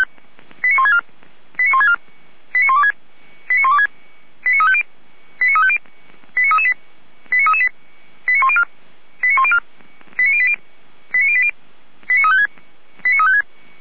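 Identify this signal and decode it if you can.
Five-tone selective call (ZVEI), heard as quick runs of five tones, tones of about 70 ms. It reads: 89164 89164 89126 89126 89350 89350 89338 89338 89104 89104 89080 89080 89356 89356
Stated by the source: Recorded with an SDR, ZVEI five-tone selective call.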